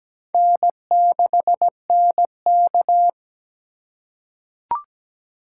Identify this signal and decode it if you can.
Morse code, heard as a keyed tone, about 17 wpm, 700 Hz. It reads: N6NK
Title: USA Call Signs - 4 Characters - 2x Character Spacing - 17wpm